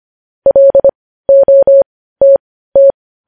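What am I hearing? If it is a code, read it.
Morse code, heard as a keyed tone, about 25 wpm, spaced as for 15 wpm, 550 Hz. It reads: LOTT
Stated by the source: Morse code news headlines at 15 wpm.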